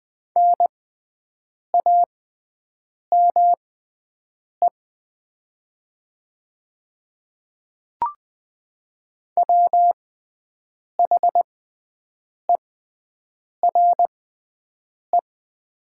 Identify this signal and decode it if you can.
Morse code, heard as a keyed tone, about 20 wpm, spaced as for 7 wpm, 700 Hz. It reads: NAME WHERE
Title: 6x Character Spacing - Top 200 Words - 20wpm